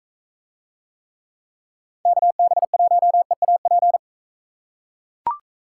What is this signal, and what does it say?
Morse code, heard as a keyed tone, about 40 wpm, 700 Hz. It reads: KB1EAP